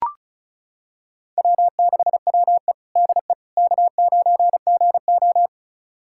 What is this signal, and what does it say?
Morse code, heard as a keyed tone, about 36 wpm, 700 Hz. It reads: W6WE DE K9GO
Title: USA Callsign DE Callsign - 4 Characters - 35wpm